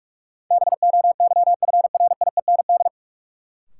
Morse code, heard as a keyed tone, about 46 wpm, 700 Hz. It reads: BOYFRIEND
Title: Sentences from Top 1000 Words - Review Part 5 - New Word Drill - 45wpm